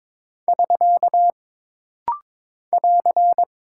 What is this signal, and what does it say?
Morse code, heard as a keyed tone, about 22 wpm, 700 Hz. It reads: <SK> <AR>
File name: practice / 22wpm / Single Punctuation-Prosign - Rapid-Fire - 22wpm